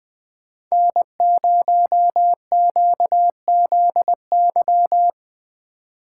Morse code, heard as a keyed tone, about 20 wpm, 700 Hz. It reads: N0QZY
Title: USA Call Signs - 3x Character Spacing - 20wpm